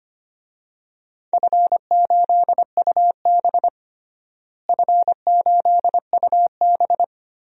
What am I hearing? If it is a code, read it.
Morse code, heard as a keyed tone, about 25 wpm, 700 Hz. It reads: F8UB F8UB
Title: DXCC-100 Call Signs - 4 Characters - Repeated 2x - 25wpm